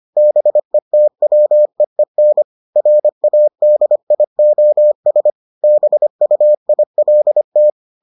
Morse code, heard as a keyed tone, about 25 wpm, 600 Hz. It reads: BETWEEN RADIOS BUILT